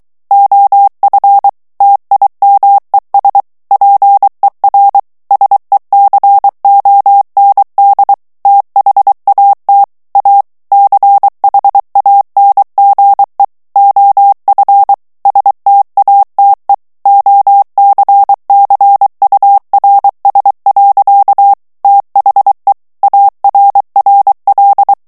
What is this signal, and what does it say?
Morse code, heard as a keyed tone, about 23 wpm, 800 Hz. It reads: OF TIMES PER SECOND THAT A CHANGE OF STATE OCCURS. THE ARRL
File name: ANGLAIS texte clair 3 vitesse 22 mots minute